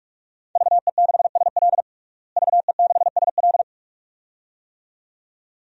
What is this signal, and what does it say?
Morse code, heard as a keyed tone, about 47 wpm, 700 Hz. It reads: VE6SL VE6SL
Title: DXCC-100 Call Signs - 5 Characters - Repeated 2x - 45wpm